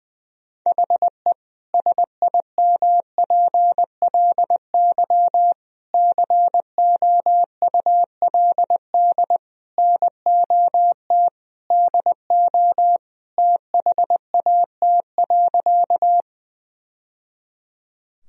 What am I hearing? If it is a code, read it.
Morse code, heard as a keyed tone, about 21 wpm, 700 Hz. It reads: HE SIMPLY COULD NOT DO THAT.
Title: Sentences from Top 1000 Words - Review Part 4 - 20wpm